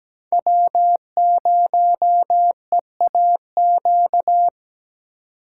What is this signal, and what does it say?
Morse code, heard as a keyed tone, about 20 wpm, 700 Hz. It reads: W0EAQ
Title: Speed Racing - USA 1x3 Call Signs 17wpm